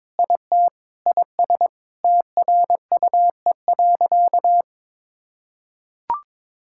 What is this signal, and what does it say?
Morse code, heard as a keyed tone, about 22 wpm, 700 Hz. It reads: IT IS TRUE.
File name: Sentences with 7-Word Phrases - 22wpm